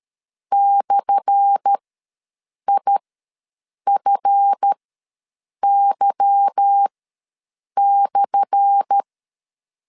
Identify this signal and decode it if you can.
Morse code, heard as a keyed tone, about 13 wpm, spaced as for 7 wpm, 800 Hz. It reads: /IFY/
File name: LEZIONI / 10 SIGNS / 10A SLASH ws 12wpm 3443